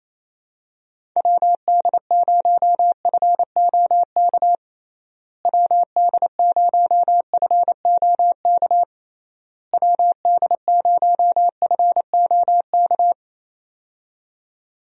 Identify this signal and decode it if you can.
Morse code, heard as a keyed tone, about 28 wpm, 700 Hz. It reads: WD0FOK WD0FOK WD0FOK
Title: USA Call Signs - 6 Characters - Repeated 3x - 28wpm